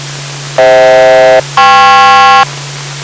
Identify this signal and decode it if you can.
Aircraft SELCAL, heard as two pairs of tones. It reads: FH-KP